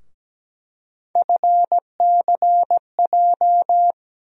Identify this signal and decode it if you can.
Morse code, heard as a keyed tone, about 17 wpm, 700 Hz. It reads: FCJ
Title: USA Call Sign Postfixes - 3 Letters 17wpm